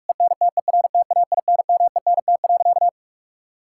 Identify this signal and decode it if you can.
Morse code, heard as a keyed tone, about 46 wpm, 700 Hz. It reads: ENTERTAINMENT.
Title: Sentences from Top 1000 Words - Review Part 6 - 45wpm